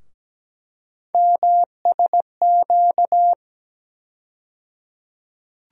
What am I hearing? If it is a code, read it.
Morse code, heard as a keyed tone, about 17 wpm, 700 Hz. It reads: MSQ
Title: USA Call Sign Postfixes - 3 Letters 17wpm